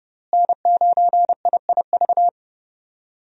Morse code, heard as a keyed tone, about 30 wpm, 700 Hz. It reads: N9IIV
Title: USA Call Signs - 5 Characters - Repeated 2x - 30wpm